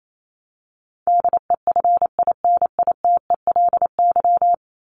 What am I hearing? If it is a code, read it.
Morse code, heard as a keyed tone, about 28 wpm, 700 Hz. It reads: DEFINITELY
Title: Top 1751-2000 Words 28wpm